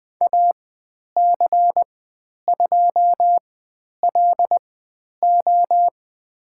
Morse code, heard as a keyed tone, about 20 wpm, spaced as for 10 wpm, 700 Hz. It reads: AC2LO
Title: 102 Lesson 17g - Letter u - Call Signs 20wpm